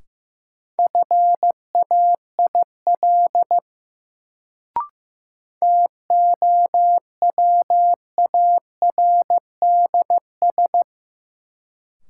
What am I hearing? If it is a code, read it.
Morse code, heard as a keyed tone, about 15 wpm, 700 Hz. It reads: FAIL TOWARDS
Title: Sentences from Top 1100 Words - New Word Drill Alt - 15wpm